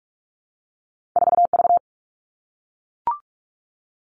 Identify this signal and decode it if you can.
Morse code, heard as a keyed tone, about 44 wpm, 700 Hz. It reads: 4V
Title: All Cut Numbers - 2-Digit 45wpm